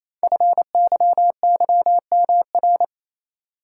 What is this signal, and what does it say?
Morse code, heard as a keyed tone, about 27 wpm, 700 Hz. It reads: FYYMR
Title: Code Groups - 5 Letter:Numbers - 28wpm